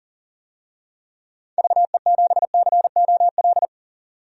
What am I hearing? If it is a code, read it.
Morse code, heard as a keyed tone, about 38 wpm, 700 Hz. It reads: VE7COL
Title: DXCC-100 Call Signs - 6 Characters - 40wpm